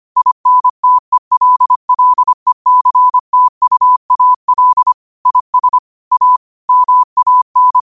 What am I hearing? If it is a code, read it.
Morse code, heard as a keyed tone, about 25 wpm, 1000 Hz. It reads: INTELLECTUAL IS A MAN
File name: qotdfiles / 25wpm